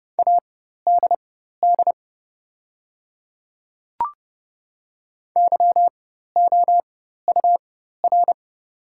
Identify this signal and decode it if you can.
Morse code, heard as a keyed tone, about 30 wpm, spaced as for 14 wpm, 700 Hz. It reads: ADD YOUR
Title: 4x Character Spacing - Top 200 Words - 30wpm